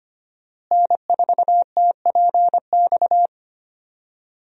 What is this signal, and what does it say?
Morse code, heard as a keyed tone, about 25 wpm, 700 Hz. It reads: N4TPX
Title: USA Call Signs - 5 Characters - Repeated 3x - 25wpm